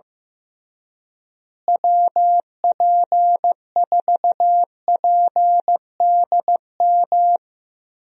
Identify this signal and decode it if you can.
Morse code, heard as a keyed tone, about 15 wpm, 700 Hz. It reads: WP4PDM
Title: USA Call Signs - 6 Characters - Repeated 3x - 15wpm